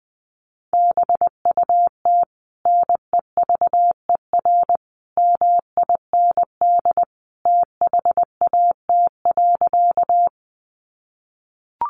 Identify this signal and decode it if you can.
Morse code, heard as a keyed tone, about 19 wpm, 700 Hz. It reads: BUT NEVER MIND THAT.
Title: Sentences from Top 500 Words 20wpm